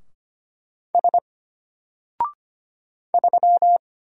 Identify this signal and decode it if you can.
Morse code, heard as a keyed tone, about 25 wpm, 700 Hz. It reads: S 3
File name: Single-Letter-Number-25wpm